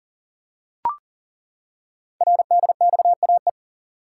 Morse code, heard as a keyed tone, about 39 wpm, 690 Hz. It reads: RDXAE
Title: Code Groups - 5 Letters - 40wpm